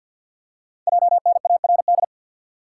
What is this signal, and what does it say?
Morse code, heard as a keyed tone, about 48 wpm, 690 Hz. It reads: JNARD
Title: Code Groups - 5 Letters - 50wpm